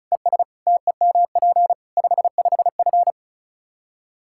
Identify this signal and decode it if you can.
Morse code, heard as a keyed tone, about 36 wpm, 700 Hz. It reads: ES TEMP 55F
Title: Practice QSO - Exchange 2 - Info - 35wpm